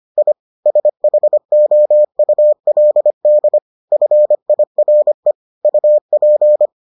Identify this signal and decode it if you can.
Morse code, heard as a keyed tone, about 25 wpm, 600 Hz. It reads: I SHOULD FIRE UP